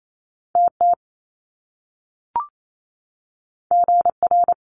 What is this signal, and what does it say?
Morse code, heard as a keyed tone, about 28 wpm, 700 Hz. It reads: TT GR
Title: Top Two Letter Combinations in Words 28wpm